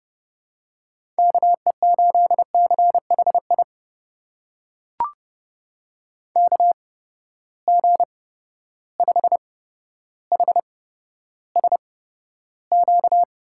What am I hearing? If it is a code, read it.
Morse code, heard as a keyed tone, about 28 wpm, spaced as for 13 wpm, 700 Hz. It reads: KE8CHI KG5HSQ
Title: USA Call Signs - 6 Characters - 8x Character Spacing - 30wpm